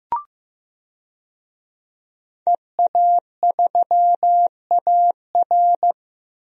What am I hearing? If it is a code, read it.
Morse code, heard as a keyed tone, about 15 wpm, 700 Hz. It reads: EA3AR
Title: DXCC-100 Call Signs - 5 Characters - 15wpm